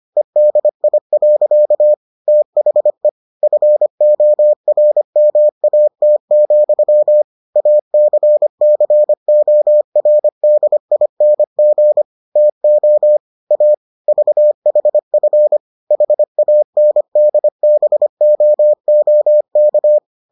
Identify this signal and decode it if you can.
Morse code, heard as a keyed tone, about 25 wpm, 600 Hz. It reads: EDI. THE FORMAT, ACCORDING TO A VHF HANDBOOK